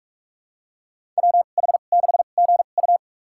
Morse code, heard as a keyed tone, about 46 wpm, spaced as for 33 wpm, 700 Hz. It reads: WH6GU
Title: USA Call Signs - 2x Character Spacing - 45wpm